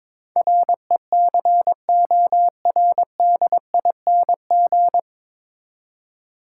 Morse code, heard as a keyed tone, about 22 wpm, 700 Hz. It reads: RECORDING
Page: Morse Code Ninja practice 22wpm